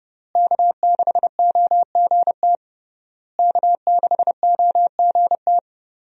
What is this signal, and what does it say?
Morse code, heard as a keyed tone, about 30 wpm, 700 Hz. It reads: K6OGT K6OGT
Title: USA Call Signs - 5 Characters - Repeated 3x - 30wpm